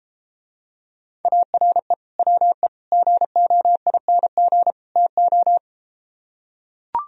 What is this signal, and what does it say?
Morse code, heard as a keyed tone, about 33 wpm, 700 Hz. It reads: ARE WE GOING TO